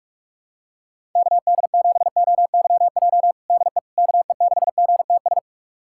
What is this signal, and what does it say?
Morse code, heard as a keyed tone, about 46 wpm, 700 Hz. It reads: KD7OYJ DE KE6GTS
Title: USA Callsign DE Callsign - 6 Characters - 45wpm